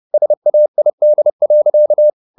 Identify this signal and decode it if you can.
Morse code, heard as a keyed tone, about 30 wpm, 600 Hz. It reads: SAID.